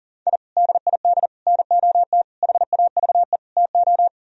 Morse code, heard as a keyed tone, about 52 wpm, 700 Hz. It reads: I DID NOT HAVE TO